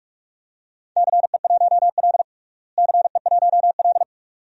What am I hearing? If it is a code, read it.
Morse code, heard as a keyed tone, about 46 wpm, 700 Hz. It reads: CE1L CE1L
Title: DXCC-100 Call Signs - 4 Characters - Repeated 2x - 45wpm